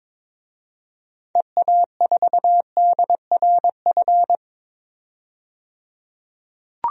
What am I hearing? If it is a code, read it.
Morse code, heard as a keyed tone, about 22 wpm, 700 Hz. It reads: EA4DRF
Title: DXCC-100 Call Signs - 6 Characters - 22wpm